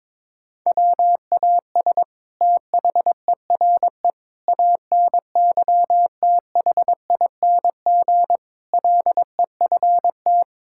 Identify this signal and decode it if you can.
Morse code, heard as a keyed tone, about 22 wpm, 700 Hz. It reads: WAS THERE ANYTHING LEFT
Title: Sentences from Top 1000 Words - Review Part 6 - 22wpm